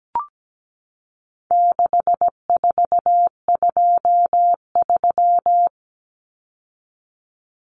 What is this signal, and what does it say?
Morse code, heard as a keyed tone, about 17 wpm, 700 Hz. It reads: 6423
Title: Numbers - 4-Digit 17wpm